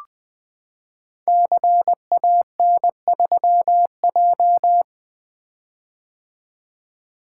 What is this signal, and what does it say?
Morse code, heard as a keyed tone, about 20 wpm, 700 Hz. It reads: CAN3J
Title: Code Groups - 5 Letter:Numbers - 20wpm